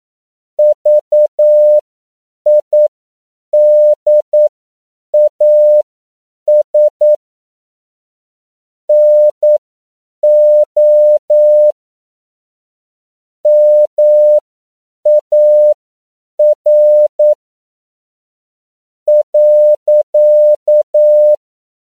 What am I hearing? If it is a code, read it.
Morse code, heard as a keyed tone, about 9 wpm, spaced as for 6 wpm, 600 Hz. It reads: VIDAS NO MAR.
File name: CW / 10PPM